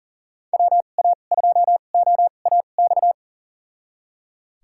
Morse code, heard as a keyed tone, about 38 wpm, spaced as for 28 wpm, 700 Hz. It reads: WA2OAX
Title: USA Call Signs - 6 Characters - 2x Character Spacing - 40wpm